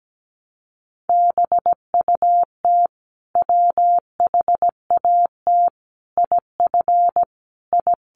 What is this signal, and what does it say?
Morse code, heard as a keyed tone, about 17 wpm, 700 Hz. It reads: BUT WHAT IF I
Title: Sentences from Top 100 Words 17wpm